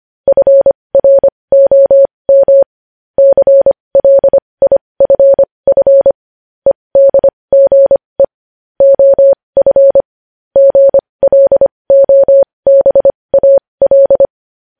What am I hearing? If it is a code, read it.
Morse code, heard as a keyed tone, about 25 wpm, spaced as for 20 wpm, 550 Hz. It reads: FROM CLIFF EDGE OF GLOBAL
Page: news headlines in morse code at 20 wpm